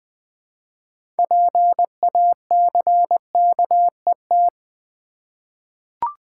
Morse code, heard as a keyed tone, about 20 wpm, 700 Hz. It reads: PACKET
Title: 6 Letter Words - First 3 Letters Given - 20wpm